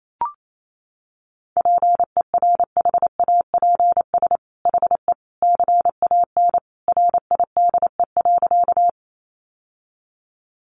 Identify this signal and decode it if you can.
Morse code, heard as a keyed tone, about 28 wpm, 700 Hz. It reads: PERHAPS HE CAN RIDE.
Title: Sentences from Top 1200 Words 28wpm